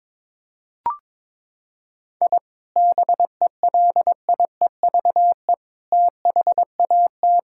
Morse code, heard as a keyed tone, about 22 wpm, 700 Hz. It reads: I BELIEVE THAT
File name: practice / 22wpm / Sentences with 4-Word Phrases - 22wpm